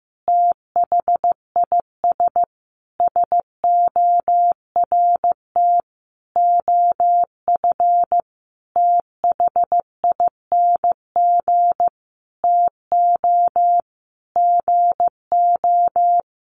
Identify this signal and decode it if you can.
Morse code, heard as a keyed tone, about 15 wpm, 700 Hz. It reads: THIS SORT OF THING TO GO
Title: Sentences from Top 1300 Words 15wpm